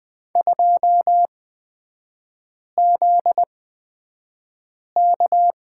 Morse code, heard as a keyed tone, about 20 wpm, spaced as for 5 wpm, 700 Hz. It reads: 2ZK
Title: 243 Lesson 39e - Letter z - Three Characters 20wpm